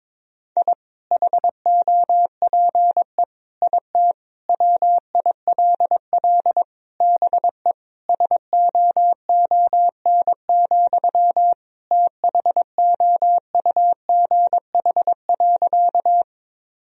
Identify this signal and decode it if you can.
Morse code, heard as a keyed tone, about 22 wpm, 700 Hz. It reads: I HOPE IT WILL BE SOON, THOUGH.
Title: Sentences from Top 1000 Words - Review Part 2 - 22wpm